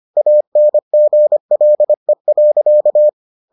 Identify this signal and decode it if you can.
Morse code, heard as a keyed tone, about 25 wpm, 600 Hz. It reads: ANGLE.